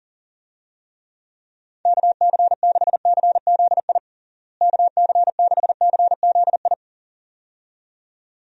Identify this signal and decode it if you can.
Morse code, heard as a keyed tone, about 40 wpm, 700 Hz. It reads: KC6CZI KC6CZI
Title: USA Call Signs - 6 Characters - Repeated 2x - 40wpm